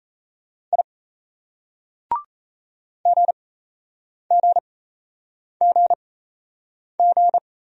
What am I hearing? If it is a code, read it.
Morse code, heard as a keyed tone, about 38 wpm, spaced as for 8 wpm, 700 Hz. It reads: I GGGG